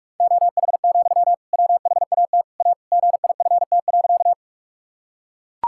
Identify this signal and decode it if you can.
Morse code, heard as a keyed tone, about 43 wpm, 700 Hz. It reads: OH, WHAT A GIFT.
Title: Sentences from Top 1000 Words - Review Part 6 - 45wpm